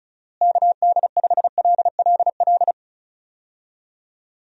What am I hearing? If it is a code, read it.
Morse code, heard as a keyed tone, about 34 wpm, 700 Hz. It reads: KD5LLL
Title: USA Call Signs - 6 Characters - Repeated 3x - 35wpm